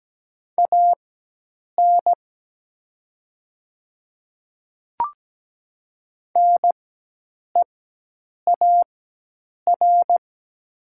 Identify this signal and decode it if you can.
Morse code, heard as a keyed tone, about 17 wpm, spaced as for 8 wpm, 700 Hz. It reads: AN NEAR
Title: 4x Character Spacing - Top 200 Words - 17wpm